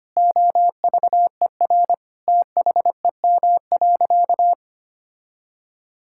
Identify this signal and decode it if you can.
Morse code, heard as a keyed tone, about 25 wpm, 700 Hz. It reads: OVER THEM.
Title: Sentences from Top 1000 Words - Review Part 6 - 25wpm